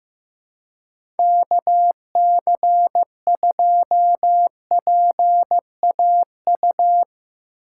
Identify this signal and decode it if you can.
Morse code, heard as a keyed tone, about 15 wpm, 700 Hz. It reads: KC2PAU
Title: USA Call Signs - 6 Characters - 6x Character Spacing - 15wpm